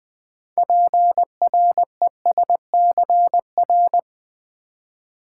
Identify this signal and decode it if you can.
Morse code, heard as a keyed tone, about 20 wpm, 700 Hz. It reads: PRESCR
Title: Build A Word - 6 Letters At A Time - 12-Letter Words - 20wpm